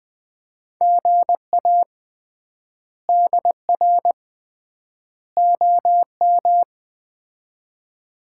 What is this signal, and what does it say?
Morse code, heard as a keyed tone, about 20 wpm, 700 Hz. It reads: GA DR OM
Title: Common QSO Phrases - 3x Word-Spacing - 20wpm